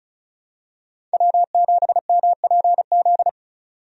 Morse code, heard as a keyed tone, about 34 wpm, 700 Hz. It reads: W7MPZ
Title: USA Call Signs - 5 Characters - 35wpm